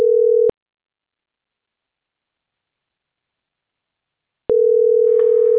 Ringback tone (North America), 440 + 480 Hz.